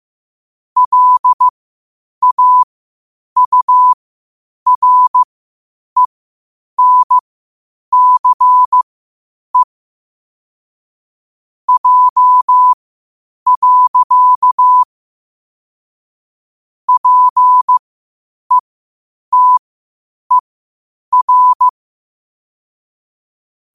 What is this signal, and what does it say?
Morse code, heard as a keyed tone, about 15 wpm, spaced as for 8 wpm, 1000 Hz. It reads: LAURENCE J. PETER
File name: qotdfiles / 8wpm